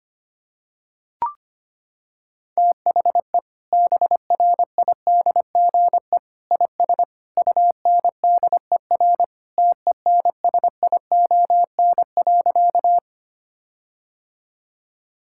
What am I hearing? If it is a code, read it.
Morse code, heard as a keyed tone, about 25 wpm, 700 Hz. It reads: THE BRIDGE IS UNDER TENSION.
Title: Sentences from Top 1600 Words 25wpm